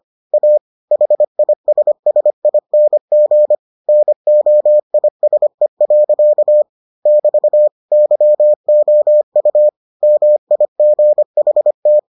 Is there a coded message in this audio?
Yes: Morse code from 0.3 to 12.0 s, heard as a keyed tone, about 25 wpm, 600 Hz. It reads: A HISSING NOISE. <BT> YOU MIGHT